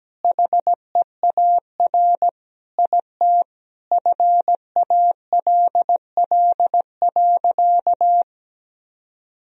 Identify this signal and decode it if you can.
Morse code, heard as a keyed tone, about 17 wpm, 700 Hz. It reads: HEAR IT FALL.
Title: Sentences from Top 1000 Words - Review Part 2 - 17wpm